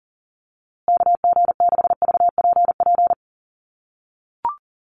Morse code, heard as a keyed tone, about 40 wpm, 700 Hz. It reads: KG6VPP
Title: USA Call Signs Repeated 3x 40wpm